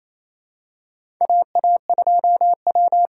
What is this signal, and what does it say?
Morse code, heard as a keyed tone, about 27 wpm, 700 Hz. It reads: AA2W
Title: USA Call Signs - 4 Characters - 2x Character Spacing - 28wpm